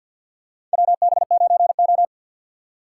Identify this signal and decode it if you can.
Morse code, heard as a keyed tone, about 52 wpm, 700 Hz. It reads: WB9O
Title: USA Call Signs - 3x Character Spacing - 50wpm